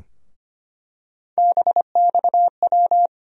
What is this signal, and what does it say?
Morse code, heard as a keyed tone, about 25 wpm, 700 Hz. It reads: BXW
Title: USA Call Sign Postfixes - 3 Letters 25wpm